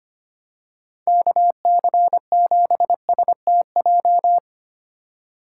Morse code, heard as a keyed tone, about 25 wpm, 700 Hz. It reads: KC7STJ
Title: USA Call Signs - 6 Characters - 4x Character Spacing - 25wpm